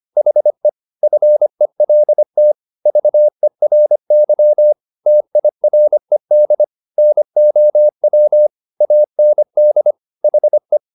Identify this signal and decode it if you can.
Morse code, heard as a keyed tone, about 25 wpm, 600 Hz. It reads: HE FELT VERY TIRED NOW AND HE
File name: the old man and the sea part 1 25wpm